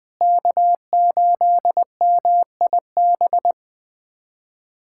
Morse code, heard as a keyed tone, about 21 wpm, 700 Hz. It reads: K8MIB
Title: USA Callsign DE Callsign - 5 Characters - 20wpm